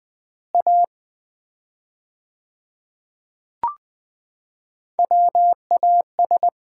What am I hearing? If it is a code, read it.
Morse code, heard as a keyed tone, about 20 wpm, 700 Hz. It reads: A WAS
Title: 4x Word-Spacing - Sets of 5 Words - 20wpm